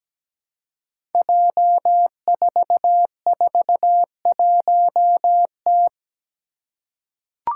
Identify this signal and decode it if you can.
Morse code, heard as a keyed tone, about 17 wpm, 700 Hz. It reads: J441T
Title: Five Challenging Letter-Number Combinations 17wpm